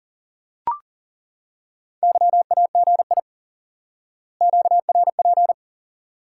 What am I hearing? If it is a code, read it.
Morse code, heard as a keyed tone, about 37 wpm, 700 Hz. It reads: YAGI QRP
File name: IQE Course 3 - Lesson 89 - YAGI - Two Elements - 40wpm